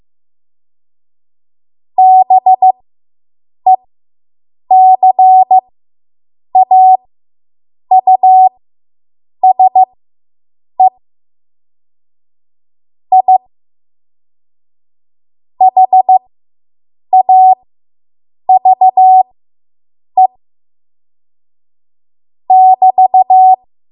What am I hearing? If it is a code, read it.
Morse code, heard as a keyed tone, about 15 wpm, spaced as for 7 wpm, 750 Hz. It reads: BECAUSE I HAVE <BT>